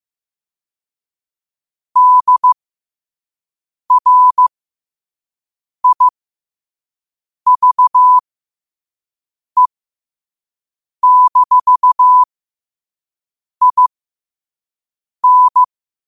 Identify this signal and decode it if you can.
Morse code, heard as a keyed tone, about 15 wpm, spaced as for 5 wpm, 1000 Hz. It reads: DRIVE-IN
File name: qotdfiles / 5wpm